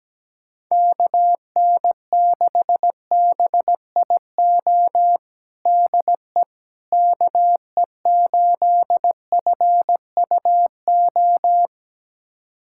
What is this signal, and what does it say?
Morse code, heard as a keyed tone, about 17 wpm, 700 Hz. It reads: KN6BIO DE KE8FUO
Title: USA Callsign DE Callsign - 6 Characters - 17wpm